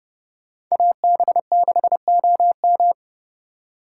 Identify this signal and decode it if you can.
Morse code, heard as a keyed tone, about 29 wpm, 690 Hz. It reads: AB6OM
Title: USA Call Signs - 5 Characters - 2x Character Spacing - 30wpm